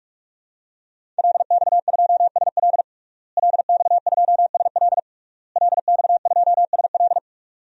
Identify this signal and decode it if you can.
Morse code, heard as a keyed tone, about 46 wpm, 700 Hz. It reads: LX2SL LX2SL LX2SL
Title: DXCC-100 Call Signs - 5 Characters - Repeated 3x - 45wpm